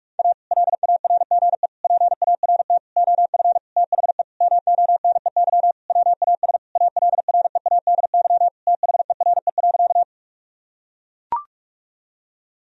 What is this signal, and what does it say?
Morse code, heard as a keyed tone, about 43 wpm, 700 Hz. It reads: A LARGE PART OF THE MONEY WAS ALREADY THERE.
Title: Sentences from Top 1000 Words - Review Part 1 - 45wpm